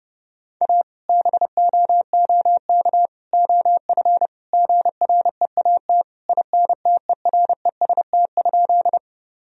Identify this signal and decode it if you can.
Morse code, heard as a keyed tone, about 30 wpm, 700 Hz. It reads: A BOOK OF GREAT INTEREST?
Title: Sentences from Top 1000 Words - Review Part 2 - 30wpm